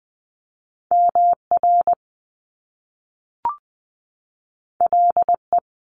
Morse code, heard as a keyed tone, about 20 wpm, 700 Hz. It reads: M R L E
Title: Top Two Letter Combinations in Words 20wpm